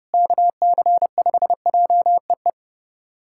Morse code, heard as a keyed tone, about 30 wpm, 700 Hz. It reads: KC5JEE
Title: USA Callsign DE Callsign - 6 Characters - 30wpm